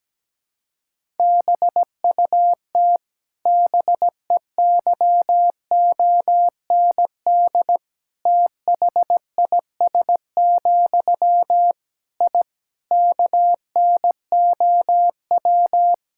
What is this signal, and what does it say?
Morse code, heard as a keyed tone, about 17 wpm, 700 Hz. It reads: BUT BEYOND THIS, I KNOW
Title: Sentences from Top 1000 Words - Review Part 6 - 17wpm